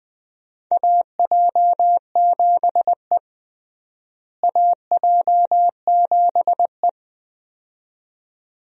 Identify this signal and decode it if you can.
Morse code, heard as a keyed tone, about 20 wpm, 700 Hz. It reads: AJ7E AJ7E
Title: USA Call Signs - 4 Characters - Repeated 2x - 20wpm